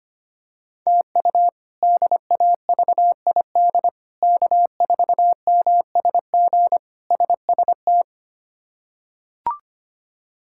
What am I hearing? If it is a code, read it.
Morse code, heard as a keyed tone, about 25 wpm, 700 Hz. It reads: TU DAVID K4MSG SST
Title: K1USN SST QSO Practice - 25wpm